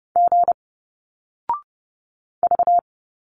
Morse code, heard as a keyed tone, about 30 wpm, 700 Hz. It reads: G V
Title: Single Letters 30wpm